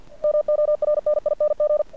Morse code, heard as a keyed tone, about 49 wpm, 600 Hz. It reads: MORNING
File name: archived / CRSnet